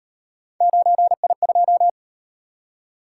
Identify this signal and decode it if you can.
Morse code, heard as a keyed tone, about 37 wpm, 700 Hz. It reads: 9I2